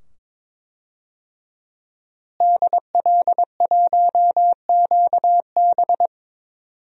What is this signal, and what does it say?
Morse code, heard as a keyed tone, about 22 wpm, 700 Hz. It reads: DL1QB
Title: DXCC-100 Call Signs - 5 Characters - Repeated 2x - 22wpm